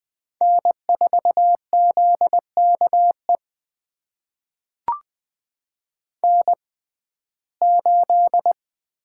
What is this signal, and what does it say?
Morse code, heard as a keyed tone, about 20 wpm, spaced as for 14 wpm, 700 Hz. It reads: N4ZKE N8
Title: USA Call Signs - 5 Characters - 6x Character Spacing - 20wpm